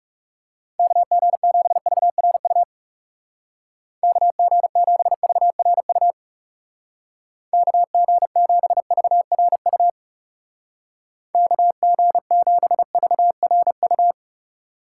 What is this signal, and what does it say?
Morse code, heard as a keyed tone, about 38 wpm, 700 Hz. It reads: KG7VRU KG7VRU KG7VRU KG7VRU